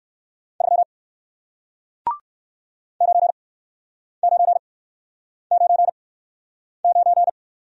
Morse code, heard as a keyed tone, about 59 wpm, spaced as for 9 wpm, 700 Hz. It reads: 3 9999